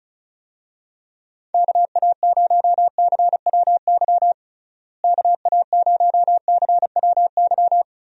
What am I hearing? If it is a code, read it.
Morse code, heard as a keyed tone, about 36 wpm, 700 Hz. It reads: KA0CWY KA0CWY
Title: USA Call Signs - 6 Characters - Repeated 3x - 35wpm